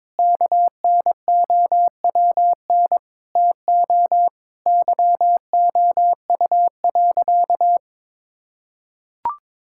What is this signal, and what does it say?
Morse code, heard as a keyed tone, about 22 wpm, 700 Hz. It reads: KNOWN TO YOU.
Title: Sentences from Top 1000 Words - Review Part 5 - 22wpm